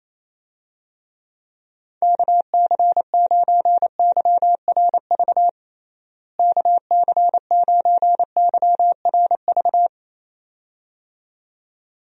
Morse code, heard as a keyed tone, about 28 wpm, 700 Hz. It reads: KC9YRV KC9YRV
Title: USA Call Signs - 6 Characters - Repeated 2x - 28wpm